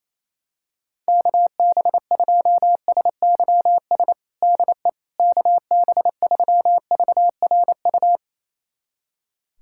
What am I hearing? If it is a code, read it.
Morse code, heard as a keyed tone, about 28 wpm, 700 Hz. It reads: KB2SYS DE KB3VRU
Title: USA Callsign DE Callsign - 6 Characters - 28wpm